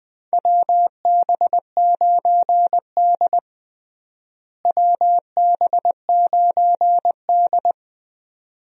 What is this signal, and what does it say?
Morse code, heard as a keyed tone, about 20 wpm, 700 Hz. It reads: WB9D WB9D
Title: USA Call Signs - 4 Characters - Repeated 2x - 20wpm